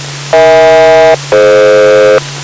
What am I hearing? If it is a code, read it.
Aircraft SELCAL, heard as two pairs of tones. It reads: EH-DF